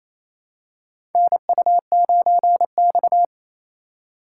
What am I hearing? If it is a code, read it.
Morse code, heard as a keyed tone, about 28 wpm, 700 Hz. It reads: NU9X
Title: USA Call Signs - 4 Characters - 3x Character Spacing - 28wpm